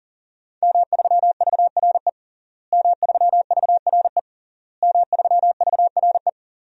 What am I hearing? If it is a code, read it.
Morse code, heard as a keyed tone, about 40 wpm, 700 Hz. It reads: M3VRE M3VRE M3VRE
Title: DXCC-100 Call Signs - 5 Characters - Repeated 3x - 40wpm